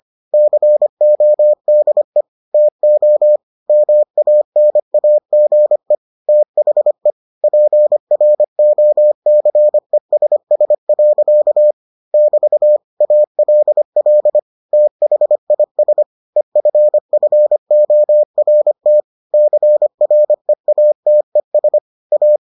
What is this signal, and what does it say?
Morse code, heard as a keyed tone, about 25 wpm, 600 Hz. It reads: CODE TO MANAGE THE PROCESS. <BT> ALL THIS EFFORT CREATES A